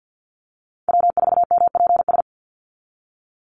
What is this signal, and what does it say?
Morse code, heard as a keyed tone, about 50 wpm, 700 Hz. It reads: W4MPS
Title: NA SOTA Chaser Callsigns 50wpm